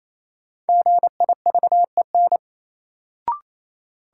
Morse code, heard as a keyed tone, about 28 wpm, 700 Hz. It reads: GIVEN
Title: Sentences from Top 1400 Words - New Word Drill - 28wpm